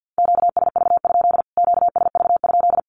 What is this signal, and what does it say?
Morse code, heard as a keyed tone, about 48 wpm, 700 Hz. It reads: QSV? QSV?
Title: Q-Codes Repeated 3x 50wpm